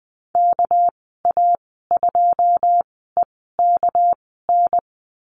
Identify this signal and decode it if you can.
Morse code, heard as a keyed tone, about 20 wpm, spaced as for 14 wpm, 700 Hz. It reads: KA2EKN
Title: 207 Lesson 33g - Letter j - Call Signs 20wpm